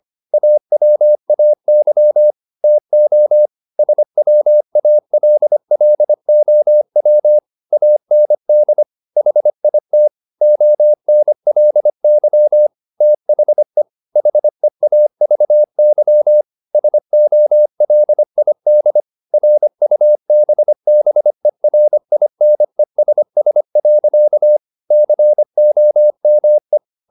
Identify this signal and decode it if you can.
Morse code, heard as a keyed tone, about 25 wpm, 600 Hz. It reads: AWAY TO SWALLOW AND HIT ONLY THE HEAVY SOLID RUBBERINESS. COME